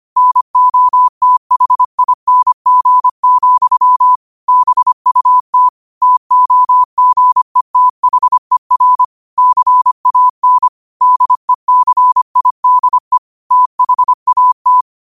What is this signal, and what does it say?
Morse code, heard as a keyed tone, about 25 wpm, 1000 Hz. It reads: NOTHING, BUT TOGETHER CAN DECIDE THAT
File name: qotdfiles / 25wpm